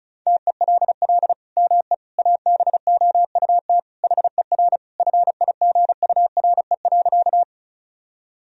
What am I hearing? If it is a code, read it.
Morse code, heard as a keyed tone, about 36 wpm, 700 Hz. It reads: TELL ME ABOUT HER FIGURE.
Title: Sentences from Top 1000 Words - Review Part 2 - 35wpm